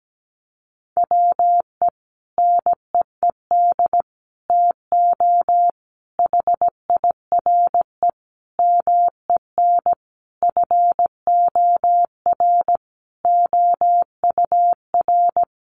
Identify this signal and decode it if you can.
Morse code, heard as a keyed tone, about 17 wpm, 700 Hz. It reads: WE NEED TO HIRE MEN FOR OUR
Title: Sentences from Top 1400 Words 17wpm